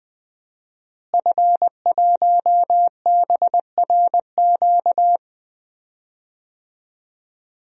Morse code, heard as a keyed tone, about 20 wpm, 700 Hz. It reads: F1BRQ
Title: DXCC-100 Call Signs - 5 Characters - Repeated 3x - 20wpm